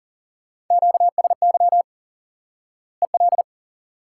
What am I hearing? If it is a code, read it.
Morse code, heard as a keyed tone, about 40 wpm, 700 Hz. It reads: QSY EL
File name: IQE Course 3 - Lesson 89 - YAGI - Two Elements - 40wpm